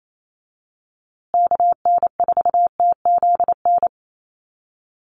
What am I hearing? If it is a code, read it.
Morse code, heard as a keyed tone, about 28 wpm, 700 Hz. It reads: KN4TZN
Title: USA Call Signs Repeated 2x 28wpm